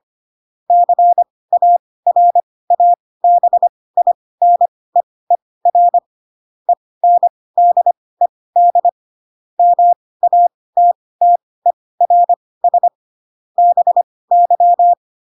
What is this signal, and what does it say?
Morse code, heard as a keyed tone, about 25 wpm, spaced as for 18 wpm, 700 Hz. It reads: CARABINEER ENDED MATTERS BY
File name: Pinocchio ch3 12 25 ik1hge